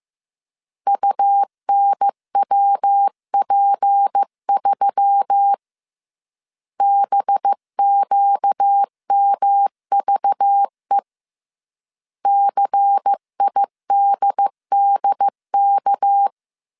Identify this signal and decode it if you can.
Morse code, heard as a keyed tone, about 15 wpm, 800 Hz. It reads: UNWP3 BQMVE CIDDK